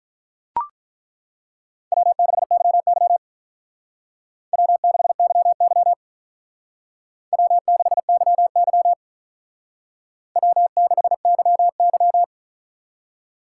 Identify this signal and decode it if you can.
Morse code, heard as a keyed tone, about 42 wpm, 700 Hz. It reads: W6YY W6YY W6YY W6YY
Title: Speed Racing - USA Call Signs 35wpm